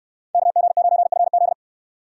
Morse code, heard as a keyed tone, about 69 wpm, 690 Hz. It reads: KG9FZ